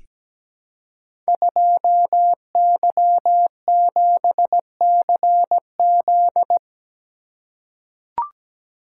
Morse code, heard as a keyed tone, about 17 wpm, 700 Hz. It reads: 2Y7CZ
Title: Code Groups - 5 Letter:Numbers - 17wpm